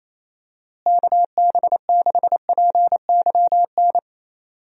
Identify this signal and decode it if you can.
Morse code, heard as a keyed tone, about 28 wpm, 700 Hz. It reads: KB6PYN
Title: USA Call Signs - 6 Characters - Repeated 2x - 28wpm